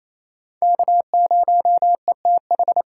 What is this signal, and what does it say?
Morse code, heard as a keyed tone, about 36 wpm, 700 Hz. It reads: K0ETH